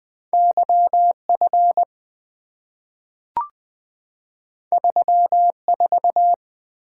Morse code, heard as a keyed tone, about 20 wpm, 700 Hz. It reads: YF 34
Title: Two Letter-Number Combinations - 20wpm